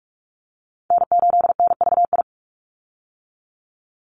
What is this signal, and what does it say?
Morse code, heard as a keyed tone, about 44 wpm, 700 Hz. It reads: N8NVI
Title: USA Call Signs 45wpm